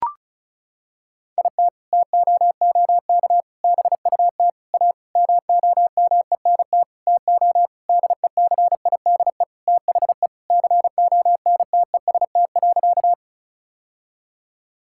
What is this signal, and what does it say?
Morse code, heard as a keyed tone, about 36 wpm, 700 Hz. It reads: IT TOOK BUT A MOMENT TO DECIDE THE CONTEST.
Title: Sentences from Top 1600 Words 35wpm